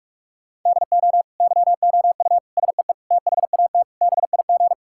Morse code, heard as a keyed tone, about 41 wpm, 700 Hz. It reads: DO YOU SEE THAT BIG
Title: Sentences from Top 1000 Words - Review Part 5 - 45wpm